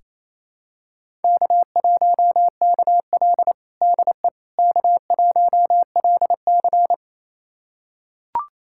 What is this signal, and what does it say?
Morse code, heard as a keyed tone, about 28 wpm, 700 Hz. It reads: K1KL DE K1LC
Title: USA Callsign DE Callsign - 4 Characters - 28wpm